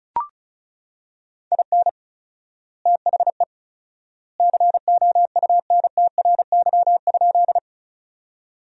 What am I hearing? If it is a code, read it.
Morse code, heard as a keyed tone, about 34 wpm, 700 Hz. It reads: IN THE COUNTRY?
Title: 4x Word-Spacing - Sentences from Top 300 Words - 35wpm